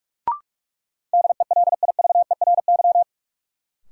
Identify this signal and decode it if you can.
Morse code, heard as a keyed tone, about 44 wpm, 700 Hz. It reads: DELIVERY
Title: Sentences from Top 1500 Words - New Word Drill Alt - 45wpm